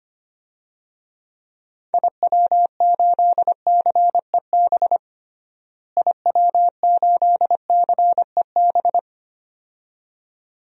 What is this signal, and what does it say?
Morse code, heard as a keyed tone, about 25 wpm, 700 Hz. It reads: IW8CEB IW8CEB
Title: DXCC-100 Call Signs - 6 Characters - Repeated 2x - 25wpm